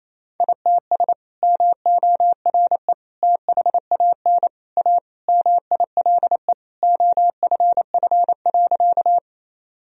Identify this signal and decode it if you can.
Morse code, heard as a keyed tone, about 28 wpm, 700 Hz. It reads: ITS MORE THAN A MILE OFF.
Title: Sentences from Top 1000 Words - Review Part 1 - 28wpm